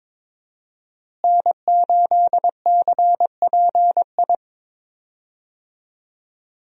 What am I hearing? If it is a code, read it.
Morse code, heard as a keyed tone, about 21 wpm, 700 Hz. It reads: N8CPI
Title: USA Call Signs - 5 Characters - Repeated 2x - 22wpm